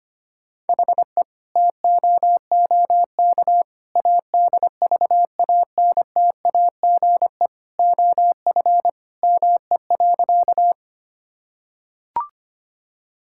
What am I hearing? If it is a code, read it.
Morse code, heard as a keyed tone, about 25 wpm, 700 Hz. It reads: HE TOOK ADVANTAGE OF ME.
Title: Sentences from Top 1000 Words - Review Part 6 - 25wpm